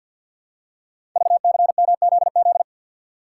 Morse code, heard as a keyed tone, about 52 wpm, 700 Hz. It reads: VCMZZ
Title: Code Groups - 5 Letters - 50wpm